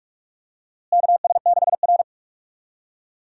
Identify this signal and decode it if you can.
Morse code, heard as a keyed tone, about 43 wpm, 700 Hz. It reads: KS6R
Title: USA Call Signs - 4 Characters - Repeated 2x - 45wpm